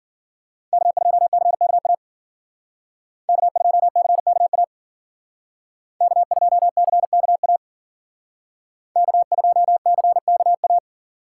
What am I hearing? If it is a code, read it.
Morse code, heard as a keyed tone, about 52 wpm, 700 Hz. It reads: K2CKA K2CKA K2CKA K2CKA